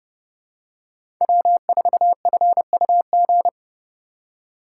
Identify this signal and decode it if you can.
Morse code, heard as a keyed tone, about 30 wpm, 700 Hz. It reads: W4FUG
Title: USA Call Signs - 2x Character Spacing - 30wpm